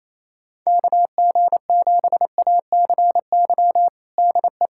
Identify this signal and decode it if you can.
Morse code, heard as a keyed tone, about 28 wpm, 700 Hz. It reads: KG7ACY DE
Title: USA Callsign DE Callsign - 6 Characters - 28wpm